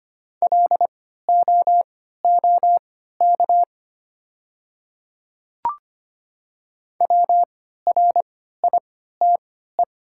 Morse code, heard as a keyed tone, about 25 wpm, spaced as for 14 wpm, 700 Hz. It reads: LOOK WRITE
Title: 3x Character Spacing - Top 200 Words - 25wpm